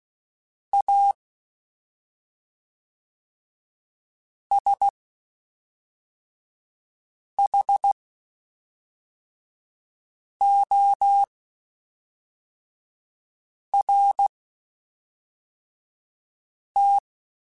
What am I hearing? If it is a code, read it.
Morse code, heard as a keyed tone, about 16 wpm, spaced as for 3 wpm, 800 Hz. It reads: A SHORT
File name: Quotes Spaced / Quotes Spaced-mp3s / Temper sp @5